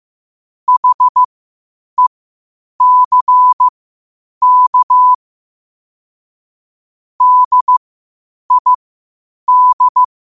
Morse code, heard as a keyed tone, about 15 wpm, spaced as for 8 wpm, 1000 Hz. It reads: HECK DID
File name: qotdfiles / 8wpm